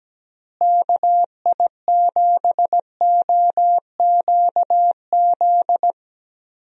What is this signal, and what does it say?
Morse code, heard as a keyed tone, about 17 wpm, 700 Hz. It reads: KI7OQZ
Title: USA Call Signs - 6 Characters - Repeated 3x - 17wpm